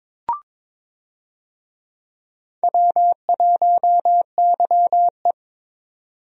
Morse code, heard as a keyed tone, about 22 wpm, 700 Hz. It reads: W1YE
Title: USA Call Signs - 4 Characters - 22wpm